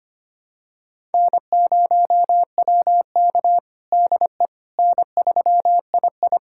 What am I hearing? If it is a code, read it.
Morse code, heard as a keyed tone, about 25 wpm, 700 Hz. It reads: N0WK DE N3II
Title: USA Callsign DE Callsign - 4 Characters - 25wpm